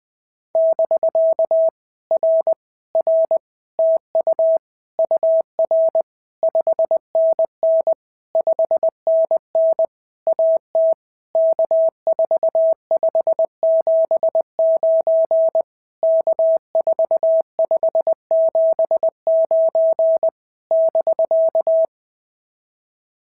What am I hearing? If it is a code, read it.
Morse code, heard as a keyed tone, about 20 wpm, 660 Hz. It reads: <BK> R R TU UR 5NN 5NN AT K4579 K4579 <BK>
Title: POTA QSO Practice - 20wpm